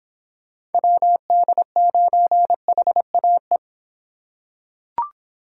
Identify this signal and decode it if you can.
Morse code, heard as a keyed tone, about 25 wpm, 700 Hz. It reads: WD9HAE